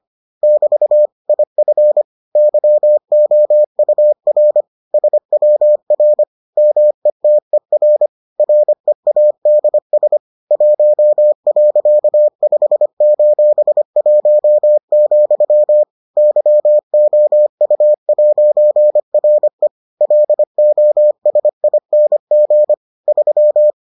Morse code, heard as a keyed tone, about 25 wpm, 600 Hz. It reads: <BT> IF YOUR SWR METER READS 1.5:1, YOU'RE LOSING 3